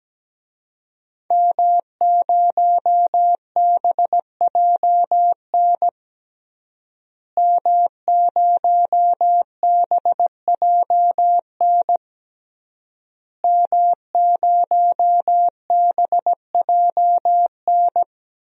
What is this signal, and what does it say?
Morse code, heard as a keyed tone, about 17 wpm, 700 Hz. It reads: M0BJN M0BJN M0BJN
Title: DXCC-100 Call Signs - 5 Characters - Repeated 3x - 17wpm